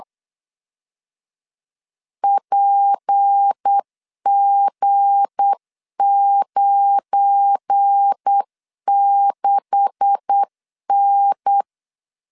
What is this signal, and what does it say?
Morse code, heard as a keyed tone, about 8 wpm, 800 Hz. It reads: PG96N
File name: LEZIONI / 9 286 8wpm 3425